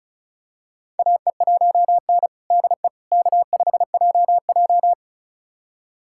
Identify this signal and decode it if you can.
Morse code, heard as a keyed tone, about 36 wpm, 700 Hz. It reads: AE1N DE K5JJ
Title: USA Callsign DE Callsign - 4 Characters - 35wpm